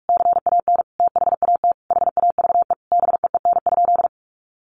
Morse code, heard as a keyed tone, about 44 wpm, 700 Hz. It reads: CAN THAT HAVE BEEN?
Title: Sentences from Top 800 Words 45wpm